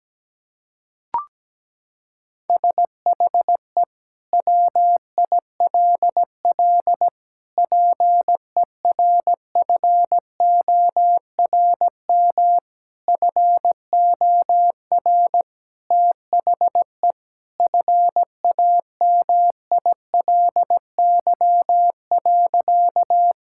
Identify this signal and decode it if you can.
Morse code, heard as a keyed tone, about 17 wpm, 700 Hz. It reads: SHE WILL PERFORM FOR THE FAMILY.
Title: Sentences from Top 1000 Words - Review Part 5 - 17wpm